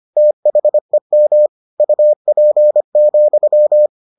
Morse code, heard as a keyed tone, about 25 wpm, 600 Hz. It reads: THEM UP,